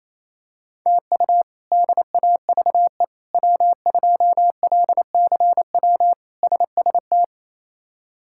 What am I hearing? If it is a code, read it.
Morse code, heard as a keyed tone, about 28 wpm, 700 Hz. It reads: TU DAVE W2LCW SST
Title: K1USN SST QSO Practice - 28wpm